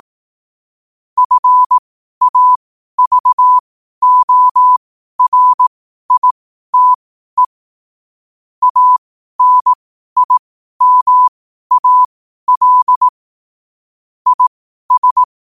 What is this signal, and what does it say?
Morse code, heard as a keyed tone, about 18 wpm, spaced as for 12 wpm, 1000 Hz. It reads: FAVORITE ANIMAL IS
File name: qotdfiles / 12wpm